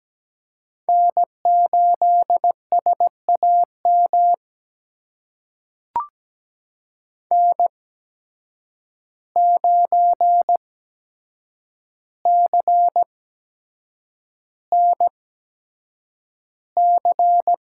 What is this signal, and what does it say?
Morse code, heard as a keyed tone, about 17 wpm, spaced as for 7 wpm, 700 Hz. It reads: N8SAM N9CNC
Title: USA Call Signs - 5 Characters - 8x Character Spacing - 17wpm